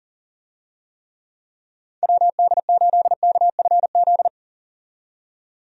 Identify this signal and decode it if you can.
Morse code, heard as a keyed tone, about 40 wpm, 700 Hz. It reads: WD8KFZ
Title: USA Call Signs - 6 Characters - 40wpm